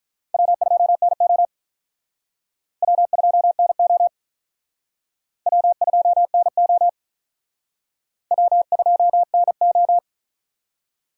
Morse code, heard as a keyed tone, about 43 wpm, 700 Hz. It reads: W2NO W2NO W2NO W2NO
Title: Speed Racing - USA Call Signs 35wpm